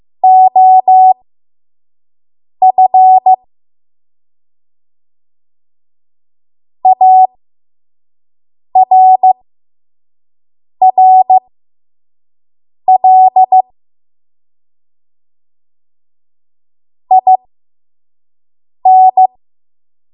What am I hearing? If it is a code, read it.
Morse code, heard as a keyed tone, about 15 wpm, spaced as for 5 wpm, 750 Hz. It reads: OF ARRL IN
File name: Morse / 05WPM